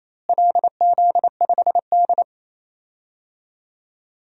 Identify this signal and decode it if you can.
Morse code, heard as a keyed tone, about 28 wpm, 700 Hz. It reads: LZ5D
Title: DXCC-100 Call Signs - 4 Characters - Repeated 2x - 28wpm